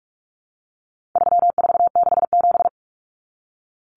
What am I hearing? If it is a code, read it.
Morse code, heard as a keyed tone, about 44 wpm, 700 Hz. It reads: 3467
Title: Numbers - 4-Digit 45wpm